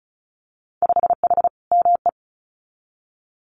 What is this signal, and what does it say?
Morse code, heard as a keyed tone, about 36 wpm, 700 Hz. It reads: 5H ME
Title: Winter Field Day Exchanges Encore 35wpm